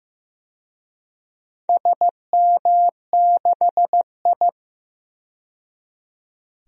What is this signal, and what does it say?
Morse code, heard as a keyed tone, about 15 wpm, 700 Hz. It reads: SM6I
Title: DXCC-100 Call Signs - 4 Characters - 15wpm